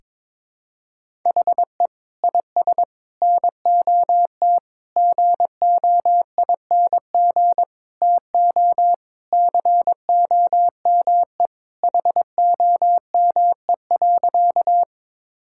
Morse code, heard as a keyed tone, about 21 wpm, 700 Hz. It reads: HE IS NOT GOING TO COME HOME.
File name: practice / 22wpm / Sentences with 4-Word Phrases - 22wpm